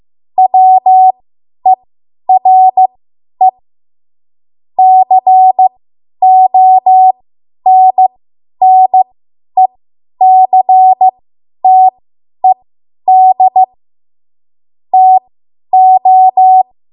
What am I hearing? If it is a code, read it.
Morse code, heard as a keyed tone, about 15 wpm, spaced as for 10 wpm, 750 Hz. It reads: WERE CONNECTED TO
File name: Morse / 10WPM